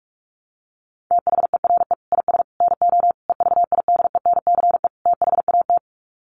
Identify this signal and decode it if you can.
Morse code, heard as a keyed tone, about 44 wpm, 700 Hz. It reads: THERE IS NO EVIDENCE THAT
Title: Sets of 5 Words Encore 45wpm